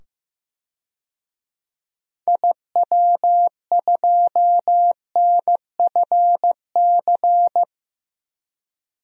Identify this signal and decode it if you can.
Morse code, heard as a keyed tone, about 15 wpm, 700 Hz. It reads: IW2NFC
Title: DXCC-100 Call Signs - 6 Characters - 15wpm